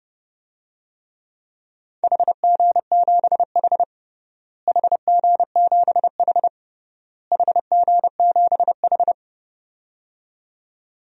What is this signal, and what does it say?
Morse code, heard as a keyed tone, about 29 wpm, 700 Hz. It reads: HG7H HG7H HG7H
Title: DXCC-100 Call Signs - 4 Characters - Repeated 3x - 30wpm